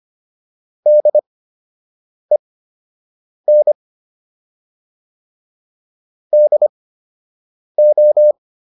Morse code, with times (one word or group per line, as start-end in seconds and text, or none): 0.9-3.7 DEN
6.3-8.3 DO